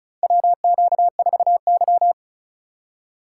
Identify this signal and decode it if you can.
Morse code, heard as a keyed tone, about 34 wpm, 690 Hz. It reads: WQ4Y
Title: USA Call Signs - 3x Character Spacing - 35wpm